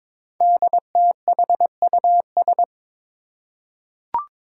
Morse code, heard as a keyed tone, about 23 wpm, 700 Hz. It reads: DTHUS
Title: Five Challenging Letter-Number Combinations 22wpm